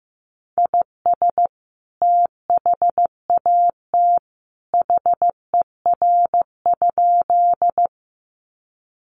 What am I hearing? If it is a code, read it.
Morse code, heard as a keyed tone, about 15 wpm, 700 Hz. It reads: IS THAT HER?
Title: Sentences from Top 100 Words 15wpm